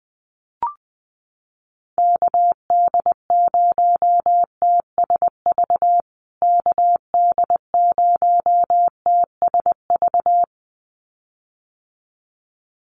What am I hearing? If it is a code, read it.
Morse code, heard as a keyed tone, about 19 wpm, 700 Hz. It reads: KD0TSV KD0TSV
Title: USA Call Signs Repeated 2x 20wpm